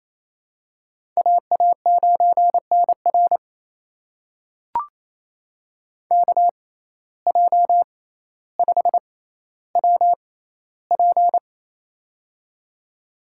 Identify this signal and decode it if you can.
Morse code, heard as a keyed tone, about 28 wpm, spaced as for 14 wpm, 700 Hz. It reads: AA9NR KJ5WP
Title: USA Call Signs - 5 Characters - 6x Character Spacing - 28wpm